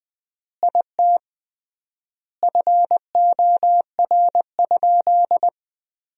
Morse code, heard as a keyed tone, about 20 wpm, 700 Hz. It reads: IT FOR?
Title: 3x Word-Spacing - Sentences from Top 100 Words - 20wpm